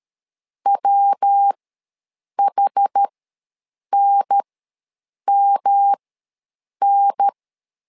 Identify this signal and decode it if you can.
Morse code, heard as a keyed tone, about 13 wpm, spaced as for 7 wpm, 800 Hz. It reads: WHNMN
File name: LEZIONI / 3 MWRO / 3 MW ws 12wpm 1816